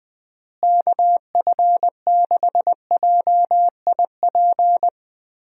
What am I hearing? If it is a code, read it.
Morse code, heard as a keyed tone, about 20 wpm, 700 Hz. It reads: KF6JIP
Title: USA Call Signs - 6 Characters - Repeated 3x - 20wpm